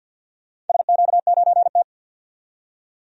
Morse code, heard as a keyed tone, about 51 wpm, 700 Hz. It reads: SQ9T
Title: DXCC-100 Call Signs - 4 Characters - 50wpm